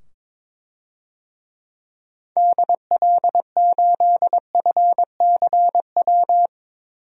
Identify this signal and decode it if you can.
Morse code, heard as a keyed tone, about 22 wpm, 700 Hz. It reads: DL8FCW
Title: DXCC-100 Call Signs - 6 Characters - 22wpm